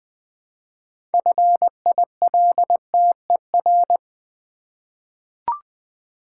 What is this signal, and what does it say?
Morse code, heard as a keyed tone, about 20 wpm, 700 Hz. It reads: FILTER
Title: 6 Letter Words - First 3 Letters Given - 20wpm